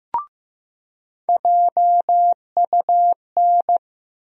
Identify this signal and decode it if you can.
Morse code, heard as a keyed tone, about 15 wpm, 700 Hz. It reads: JUN
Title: USA Call Sign Postfixes - 3 Letters 15wpm